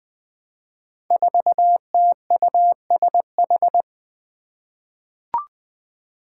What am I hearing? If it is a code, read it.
Morse code, heard as a keyed tone, about 20 wpm, 700 Hz. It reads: 4TUSH
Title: Five Challenging Letter-Number Combinations 20wpm